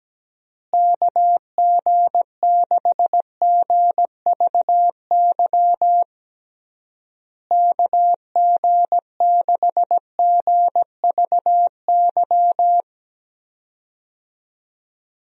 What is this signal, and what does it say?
Morse code, heard as a keyed tone, about 17 wpm, 700 Hz. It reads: KG6GVY KG6GVY
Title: USA Call Signs - 6 Characters - Repeated 3x - 17wpm